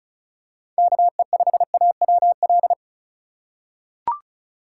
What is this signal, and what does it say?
Morse code, heard as a keyed tone, about 36 wpm, 690 Hz. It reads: KE5AWL
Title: USA Call Signs - 6 Characters - 2x Character Spacing - 35wpm